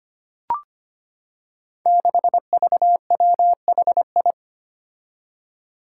Morse code, heard as a keyed tone, about 25 wpm, 700 Hz. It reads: 6VWHI
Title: Five Challenging Letter-Number Combinations 25wpm